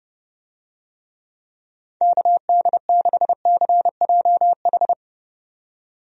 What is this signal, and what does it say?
Morse code, heard as a keyed tone, about 32 wpm, 700 Hz. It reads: KD6CJH
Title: USA Call Signs - 6 Characters - 30wpm